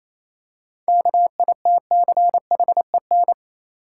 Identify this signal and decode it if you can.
Morse code, heard as a keyed tone, about 27 wpm, 690 Hz. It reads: KITCHEN
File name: Easy Words - Speed-Racing with Hints - 22wpm